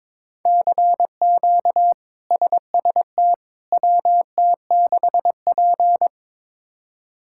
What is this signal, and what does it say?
Morse code, heard as a keyed tone, about 23 wpm, 700 Hz. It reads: CQ SST WT6P
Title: K1USN SST QSO Practice - 22wpm